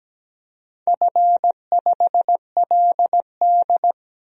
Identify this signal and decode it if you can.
Morse code, heard as a keyed tone, about 17 wpm, 700 Hz. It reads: F5LD
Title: DXCC-100 Call Signs - 4 Characters - 17wpm